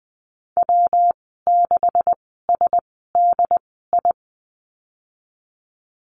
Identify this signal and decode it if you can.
Morse code, heard as a keyed tone, about 20 wpm, spaced as for 15 wpm, 700 Hz. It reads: W6SDI
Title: 147 Lesson 24g - Letter y - Call Signs 20wpm